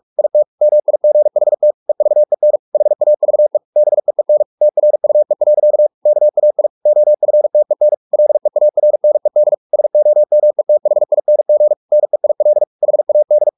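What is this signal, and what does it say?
Morse code, heard as a keyed tone, about 44 wpm, 600 Hz. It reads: IT MIGHT EVEN HAVE BEEN TRUE. KAI OFTEN LEARNED SOMETHING NEIL HAD
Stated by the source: Morse practice at 45 wpm, the novel Never by Ken Follett in English, from CW med Gnister.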